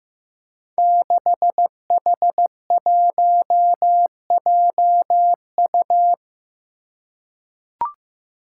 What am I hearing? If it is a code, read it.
Morse code, heard as a keyed tone, about 15 wpm, 700 Hz. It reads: 6H1JU
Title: Five Challenging Letter-Number Combinations 15wpm